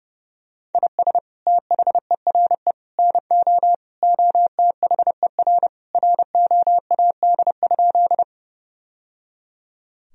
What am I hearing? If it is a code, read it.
Morse code, heard as a keyed tone, about 30 wpm, 700 Hz. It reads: IS THERE NO OTHER ROAD?
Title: Sentences from Top 1000 Words - Review Part 2 - 30wpm